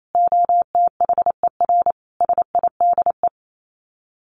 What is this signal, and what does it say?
Morse code, heard as a keyed tone, about 28 wpm, 700 Hz. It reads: OTHER SIDE
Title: Sets of 4 Words 28wpm